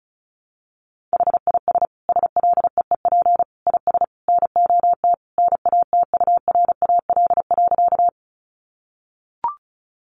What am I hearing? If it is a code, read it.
Morse code, heard as a keyed tone, about 34 wpm, 700 Hz. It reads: HIS SLEEP IS NOT NATURAL.
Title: Sentences from Top 700 Words 35wpm